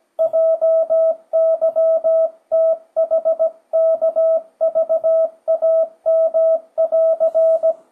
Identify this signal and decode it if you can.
Morse code, heard as a keyed tone, about 17 wpm, 650 Hz. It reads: JYTHKVAM<AR>
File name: 2025cw / speedup / 17wpm